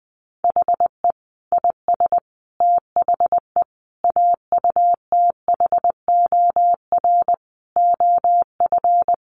Morse code, heard as a keyed tone, about 20 wpm, 700 Hz. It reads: HE IS THE AUTHOR OF
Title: Sets of 5 Words Encore2 20wpm